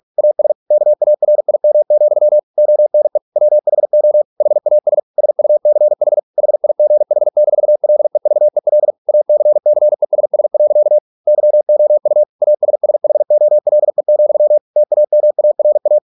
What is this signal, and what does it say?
Morse code, heard as a keyed tone, about 46 wpm, 600 Hz. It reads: AS KARIM, ONE WHO HAS SUCH HIGH-LEVEL ACCESS. YOU ASSHOLE, TAMARA